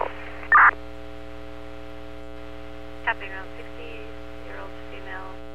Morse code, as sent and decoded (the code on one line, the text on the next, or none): none